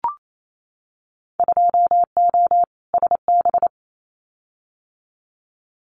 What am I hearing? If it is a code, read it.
Morse code, heard as a keyed tone, about 28 wpm, 700 Hz. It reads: 2O SB
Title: Winter Field Day Exchanges 28wpm